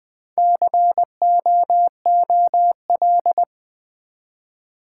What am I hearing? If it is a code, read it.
Morse code, heard as a keyed tone, about 20 wpm, 700 Hz. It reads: COOL
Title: Build A Word - 1 Letter At A Time - 3 to 5-Letter Words - 20wpm